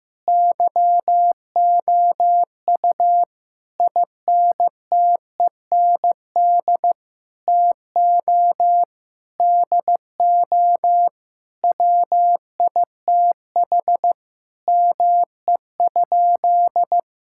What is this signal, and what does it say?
Morse code, heard as a keyed tone, about 15 wpm, 700 Hz. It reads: YOU INTEND TO DO WITH ME?
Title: Sentences from Top 1600 Words 15wpm